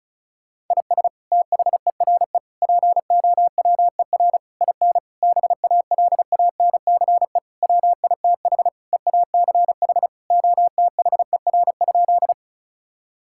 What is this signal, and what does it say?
Morse code, heard as a keyed tone, about 34 wpm, 700 Hz. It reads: IS THERE POWER IN BALANCE WITH EACH OTHER?
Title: Sentences from Top 1000 Words - Review Part 6 - 35wpm